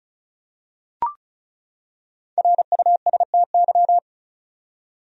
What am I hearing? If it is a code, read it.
Morse code, heard as a keyed tone, about 35 wpm, 700 Hz. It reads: RUSTY
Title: USA Common Names - Max 5 Letters - 35wpm